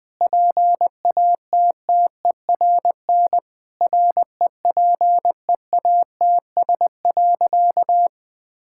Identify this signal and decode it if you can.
Morse code, heard as a keyed tone, about 20 wpm, 700 Hz. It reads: PATTERN REPEATS.
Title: Sentences with 6-Word Phrases - 20wpm